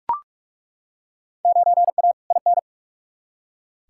Morse code, heard as a keyed tone, about 43 wpm, 700 Hz. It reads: 9A IN
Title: Field Day Exchanges Encore - 45wpm